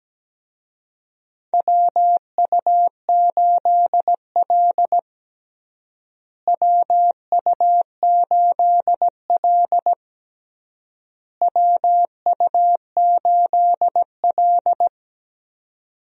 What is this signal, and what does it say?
Morse code, heard as a keyed tone, about 17 wpm, 700 Hz. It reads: WU8L WU8L WU8L
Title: USA Call Signs - 4 Characters - Repeated 3x - 17wpm